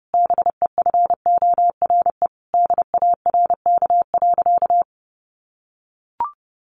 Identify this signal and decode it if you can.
Morse code, heard as a keyed tone, about 30 wpm, 700 Hz. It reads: BEFORE DARK.
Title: Sentences from Top 1100 Words 30wpm